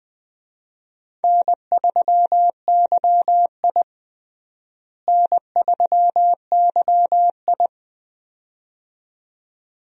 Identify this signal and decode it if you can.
Morse code, heard as a keyed tone, about 20 wpm, 700 Hz. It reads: N3YI N3YI
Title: USA Call Signs - 4 Characters - Repeated 2x - 20wpm